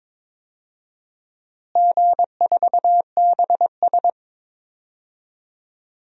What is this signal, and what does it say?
Morse code, heard as a keyed tone, about 21 wpm, 700 Hz. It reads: G4BS
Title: DXCC-100 Call Signs - 4 Characters - Repeated 3x - 22wpm